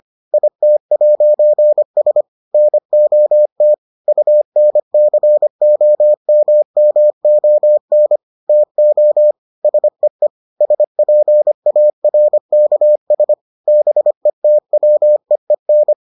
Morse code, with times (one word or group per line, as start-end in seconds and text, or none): 0.3-2.2 IT'S
2.5-3.7 NOT
4.1-8.2 UNCOMMON
8.5-9.3 TO
9.6-10.3 SEE
10.6-13.3 SPARKS
13.7-15.9 BETWEEN